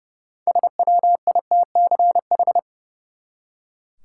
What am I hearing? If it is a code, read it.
Morse code, heard as a keyed tone, about 30 wpm, 700 Hz. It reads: SWITCH
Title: Six Letter Words - 30wpm